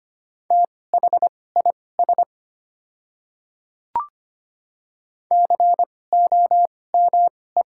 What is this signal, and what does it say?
Morse code, heard as a keyed tone, about 25 wpm, spaced as for 18 wpm, 700 Hz. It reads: THIS COME
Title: 2x Character Spacing - Top 100 Words - 25wpm